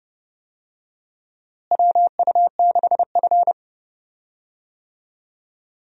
Morse code, heard as a keyed tone, about 30 wpm, 700 Hz. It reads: WU6F
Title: USA Call Signs - 4 Characters - Repeated 2x - 30wpm